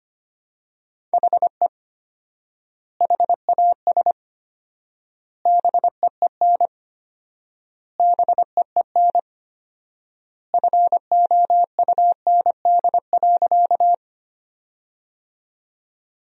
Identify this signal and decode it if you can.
Morse code, heard as a keyed tone, about 25 wpm, 700 Hz. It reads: HE HAS BEEN BEEN FOUND.
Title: 4x Word-Spacing - Sentences from Top 300 Words - 25wpm